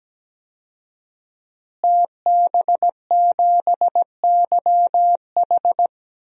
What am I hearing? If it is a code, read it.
Morse code, heard as a keyed tone, about 17 wpm, 700 Hz. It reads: TB7YH
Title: DXCC-100 Call Signs - 5 Characters - Repeated 3x - 17wpm